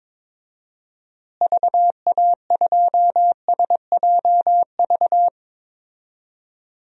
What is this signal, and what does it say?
Morse code, heard as a keyed tone, about 21 wpm, 700 Hz. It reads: VA2SJV
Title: DXCC-100 Call Signs - 6 Characters - Repeated 2x - 22wpm